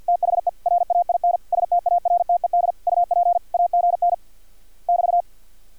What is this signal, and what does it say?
Morse code, heard as a keyed tone, about 50 wpm, 700 Hz. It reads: THE RAIN STARTED UP AGN <BT>
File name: archived / CRSnet